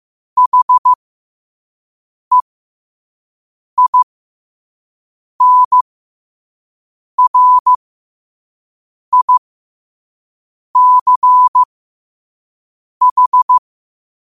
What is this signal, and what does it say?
Morse code, heard as a keyed tone, about 15 wpm, spaced as for 5 wpm, 1000 Hz. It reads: HEINRICH
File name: qotdfiles / 5wpm